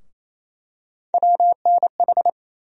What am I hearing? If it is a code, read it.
Morse code, heard as a keyed tone, about 28 wpm, 700 Hz. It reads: WNH
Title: USA Call Sign Postfixes - 3 Letters 28wpm